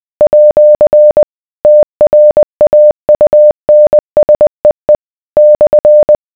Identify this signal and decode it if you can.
Morse code, heard as a keyed tone, about 20 wpm, 600 Hz. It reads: @ TRAUNSEE /